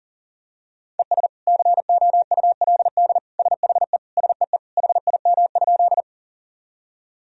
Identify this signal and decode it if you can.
Morse code, heard as a keyed tone, about 40 wpm, 700 Hz. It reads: ES COULD SHE SEE HIM?
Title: Sentences from Top 100 Words and Abbreviations 40wpm